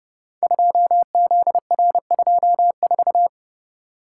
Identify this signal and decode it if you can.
Morse code, heard as a keyed tone, about 30 wpm, 700 Hz. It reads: 2ZR24
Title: Code Groups - 5 Letter:Numbers - 30wpm